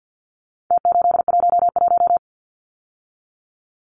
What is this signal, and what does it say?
Morse code, heard as a keyed tone, about 51 wpm, 700 Hz. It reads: T811
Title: Numbers - Contest Style - 4-Digit 50wpm